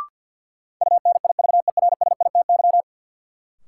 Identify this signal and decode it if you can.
Morse code, heard as a keyed tone, about 51 wpm, 700 Hz. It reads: UNIVERSITY
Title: Sentences from Top 1300 Words - New Word Drill Alt - 50wpm